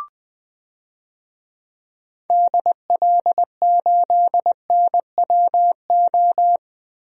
Morse code, heard as a keyed tone, about 20 wpm, 700 Hz. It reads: DL8NWO
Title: DXCC-100 Call Signs - 6 Characters - 20wpm